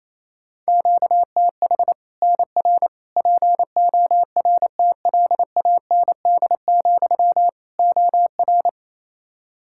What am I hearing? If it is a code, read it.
Morse code, heard as a keyed tone, about 28 wpm, 700 Hz. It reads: QTH NR PORTLAND, OR
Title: Practice QSO - Exchange 1 - Beginning - 28wpm